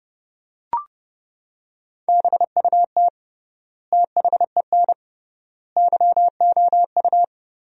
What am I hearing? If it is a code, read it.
Morse code, heard as a keyed tone, about 30 wpm, 700 Hz. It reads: BUT THEN YOU
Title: 3x Word-Spacing - Sentences from Top 300 Words - 30wpm